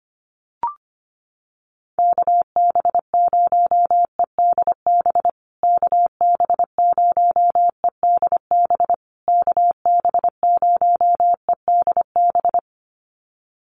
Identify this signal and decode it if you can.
Morse code, heard as a keyed tone, about 25 wpm, 700 Hz. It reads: KB0EDB KB0EDB KB0EDB
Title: USA Call Signs Repeated 3x 25wpm